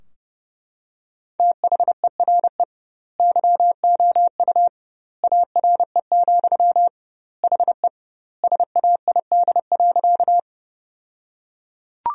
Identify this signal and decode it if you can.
Morse code, heard as a keyed tone, about 30 wpm, 700 Hz. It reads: THERE YOU ARE, HE SAID.
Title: 2x Word-Spacing - Sentences from Top 100 Words - 30wpm